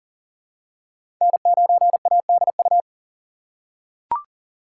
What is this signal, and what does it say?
Morse code, heard as a keyed tone, about 40 wpm, 700 Hz. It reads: N9ADU
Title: USA Call Signs - 5 Characters - 3x Character Spacing - 40wpm